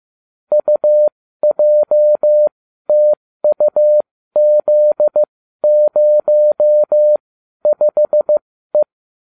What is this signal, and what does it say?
Morse code, heard as a keyed tone, about 15 wpm, spaced as for 12 wpm, 600 Hz. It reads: UJTUZ05E